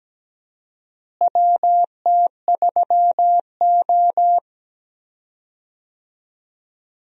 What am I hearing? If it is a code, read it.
Morse code, heard as a keyed tone, about 17 wpm, 700 Hz. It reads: WT3O
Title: USA Call Signs - 4 Characters - Repeated 2x - 17wpm